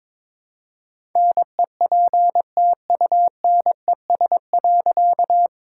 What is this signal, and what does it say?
Morse code, heard as a keyed tone, about 22 wpm, 700 Hz. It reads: NEPTUNES.
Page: morse-code-ninja-practice-22wpm